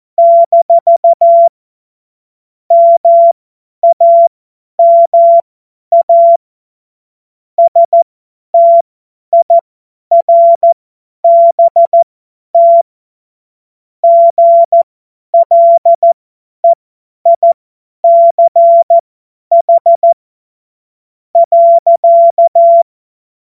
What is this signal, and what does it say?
Morse code, heard as a keyed tone, about 14 wpm, spaced as for 10 wpm, 680 Hz. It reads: - MAMA STIRBT GLEICH.
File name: Morsekurs / Witze / Witz 02. Februar 2021, 50 F70